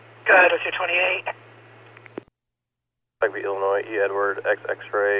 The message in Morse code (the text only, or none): none